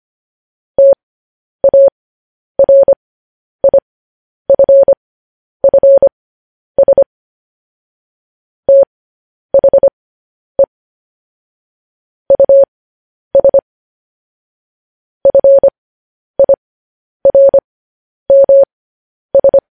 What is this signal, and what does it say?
Morse code, heard as a keyed tone, about 25 wpm, spaced as for 10 wpm, 550 Hz. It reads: TARIFFS THE US FIRMS
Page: news-headlines-in-morse-code-at-10-wpm